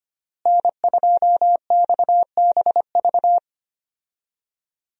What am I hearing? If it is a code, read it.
Morse code, heard as a keyed tone, about 25 wpm, 700 Hz. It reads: N2XBV
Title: USA Call Signs - 5 Characters - Repeated 3x - 25wpm